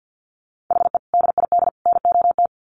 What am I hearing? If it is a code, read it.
Morse code, heard as a keyed tone, about 50 wpm, 700 Hz. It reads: HE DID NOT
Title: Sets of 3 Words 50wpm